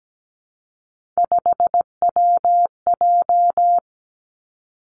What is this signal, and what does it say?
Morse code, heard as a keyed tone, about 17 wpm, 700 Hz. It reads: 5WJ
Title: Three Challenging Letter-Number Combinations 17wpm